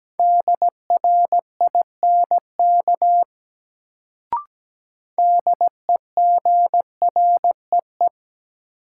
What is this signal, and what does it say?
Morse code, heard as a keyed tone, about 17 wpm, 700 Hz. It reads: DRINK DEGREE
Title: Sentences from Top 1500 Words - New Word Drill Alt - 17wpm